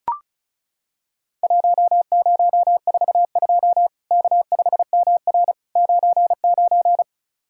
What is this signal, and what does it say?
Morse code, heard as a keyed tone, about 36 wpm, 700 Hz. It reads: 1042 K5MR 99
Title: Worked All Europe DX Contest - QTC Exchange - 35wpm